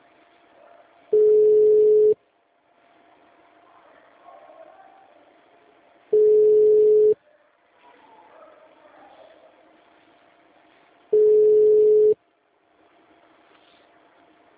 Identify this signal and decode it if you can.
Ringback tone (Europe, 425 Hz).